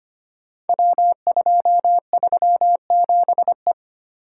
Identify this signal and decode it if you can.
Morse code, heard as a keyed tone, about 25 wpm, 690 Hz. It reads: W237E
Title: Code Groups - 5 Letter:Numbers - 25wpm